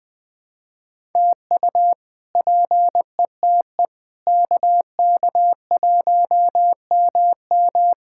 Morse code, heard as a keyed tone, about 20 wpm, 700 Hz. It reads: TU PETE KK1MM
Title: K1USN SST QSO Practice - 20wpm